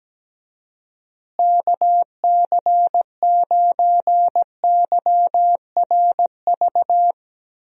Morse code, heard as a keyed tone, about 17 wpm, 700 Hz. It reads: KC9YRV
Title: USA Call Signs - 6 Characters - Repeated 2x - 17wpm